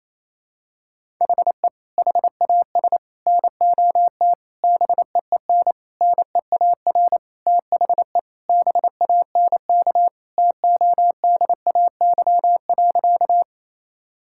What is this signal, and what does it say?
Morse code, heard as a keyed tone, about 27 wpm, 700 Hz. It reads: HE HAS NOT BEEN NEAR THE BANK TODAY.
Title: Sentences from Top 1000 Words - Review Part 6 - 28wpm